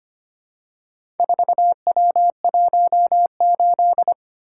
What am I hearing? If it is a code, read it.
Morse code, heard as a keyed tone, about 25 wpm, 700 Hz. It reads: 4W18
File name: practice / 25wpm / Four Letter-Number Combinations - 25wpm